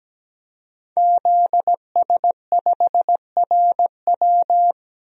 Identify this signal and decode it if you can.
Morse code, heard as a keyed tone, about 17 wpm, 700 Hz. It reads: ZS5RW
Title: DXCC-100 Call Signs - 5 Characters - Repeated 3x - 17wpm